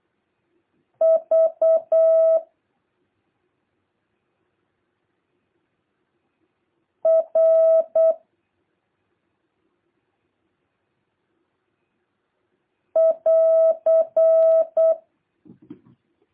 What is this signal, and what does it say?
Morse code, heard as a keyed tone, about 8 wpm, 650 Hz. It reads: V R <AR>
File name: jh8cbh / 2023cw